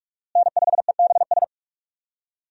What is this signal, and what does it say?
Morse code, heard as a keyed tone, about 46 wpm, 700 Hz. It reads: N5EBS
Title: USA Call Signs - 5 Characters - Repeated 3x - 45wpm